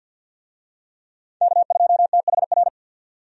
Morse code, heard as a keyed tone, about 52 wpm, 700 Hz. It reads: K2THR
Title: USA Call Signs - 5 Characters - 50wpm